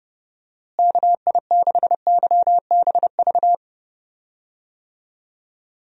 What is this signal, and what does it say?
Morse code, heard as a keyed tone, about 32 wpm, 700 Hz. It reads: KI6YBV
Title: USA Call Signs - 6 Characters - Repeated 3x - 30wpm